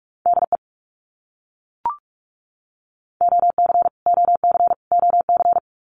Morse code, heard as a keyed tone, about 44 wpm, 700 Hz. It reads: DE OC OC OC
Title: Abbreviations Repeated 3x - 45wpm